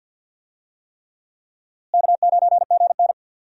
Morse code, heard as a keyed tone, about 51 wpm, 700 Hz. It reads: K9GN
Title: USA Call Signs - 4 Characters - Repeated 3x - 50wpm